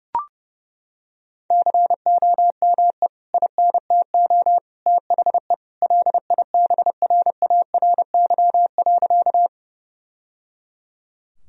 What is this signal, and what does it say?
Morse code, heard as a keyed tone, about 30 wpm, 700 Hz. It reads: COME INTO THE LIBRARY.
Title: Sentences from Top 1000 Words - Review Part 4 - 30wpm